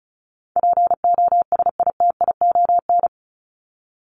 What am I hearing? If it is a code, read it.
Morse code, heard as a keyed tone, about 34 wpm, 700 Hz. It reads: POSITION
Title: Top 751-1000 Words 35wpm